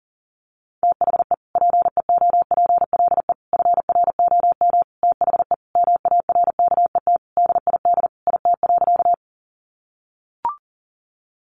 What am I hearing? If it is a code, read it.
Morse code, heard as a keyed tone, about 40 wpm, 700 Hz. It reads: THE PEOPLE FROM THE MARKET DID IT.
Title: Sentences from Top 600 Words 40wpm